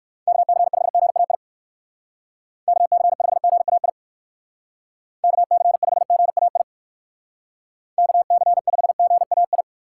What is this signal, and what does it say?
Morse code, heard as a keyed tone, about 59 wpm, 700 Hz. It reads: KC5GAI KC5GAI KC5GAI KC5GAI